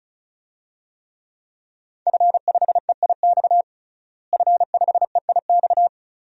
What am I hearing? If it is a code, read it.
Morse code, heard as a keyed tone, about 34 wpm, 700 Hz. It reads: F5EIX F5EIX
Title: DXCC-100 Call Signs - 5 Characters - Repeated 2x - 35wpm